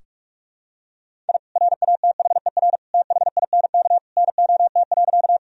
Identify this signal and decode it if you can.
Morse code, heard as a keyed tone, about 42 wpm, 700 Hz. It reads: I RATHER THINK NOT.
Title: Sentences from Top 1000 Words - Review Part 4 - 45wpm